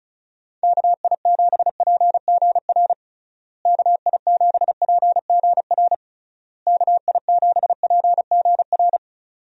Morse code, heard as a keyed tone, about 34 wpm, 700 Hz. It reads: KI7PGR KI7PGR KI7PGR
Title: USA Call Signs - 6 Characters - Repeated 3x - 35wpm